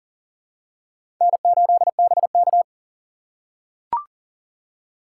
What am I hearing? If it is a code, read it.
Morse code, heard as a keyed tone, about 40 wpm, 700 Hz. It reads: N8BK
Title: USA Call Signs - 4 Characters - 2x Character Spacing - 40wpm